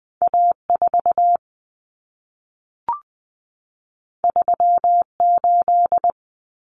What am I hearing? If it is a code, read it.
Morse code, heard as a keyed tone, about 20 wpm, 700 Hz. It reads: A4 38
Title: Common Cut Numbers - 2-Digit 20wpm